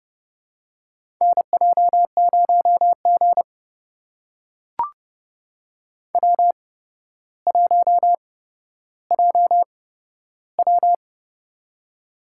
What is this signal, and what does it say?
Morse code, heard as a keyed tone, about 30 wpm, spaced as for 13 wpm, 700 Hz. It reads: NJ0G W1JW
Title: USA Call Signs - 4 Characters - 8x Character Spacing - 30wpm